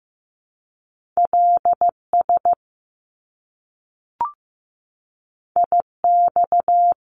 Morse code, heard as a keyed tone, about 15 wpm, 700 Hz. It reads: LS IX
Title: Top Two Letter Combinations in Words 15wpm